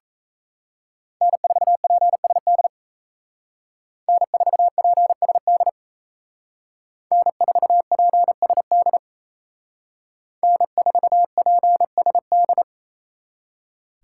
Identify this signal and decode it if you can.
Morse code, heard as a keyed tone, about 33 wpm, 700 Hz. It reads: N4PSD N4PSD N4PSD N4PSD
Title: Speed Racing - USA 1x3 Call Signs 28wpm